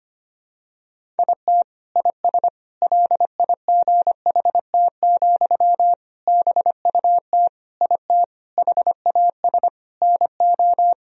Morse code, heard as a keyed tone, about 25 wpm, 700 Hz. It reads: IT IS LIGHT, BUT IT HAS NO
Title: Sentences from Top 1000 Words - Review Part 1 - 25wpm